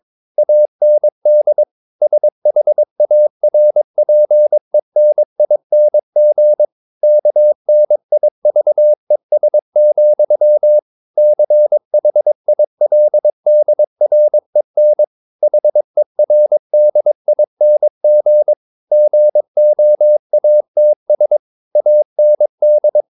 Morse code, heard as a keyed tone, about 22 wpm, 600 Hz. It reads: AND SHARPENING KNIVES, CHILDREN HERDING GOATS AND
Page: Romanen Never af Ken Follet på engelsk.